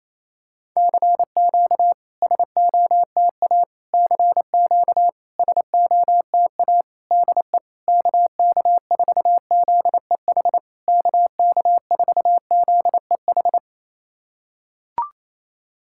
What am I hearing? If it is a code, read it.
Morse code, heard as a keyed tone, about 28 wpm, 690 Hz. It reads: CQ SOTA CQ SOTA DE KK4ZEH KK4ZEH
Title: SOTA QSO Practice 28wpm